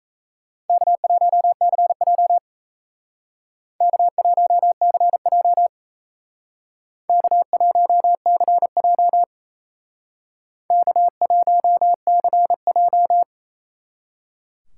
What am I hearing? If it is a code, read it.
Morse code, heard as a keyed tone, about 32 wpm, 700 Hz. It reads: K1CJ K1CJ K1CJ K1CJ
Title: Speed Racing - USA 1x2 Call Signs 28wpm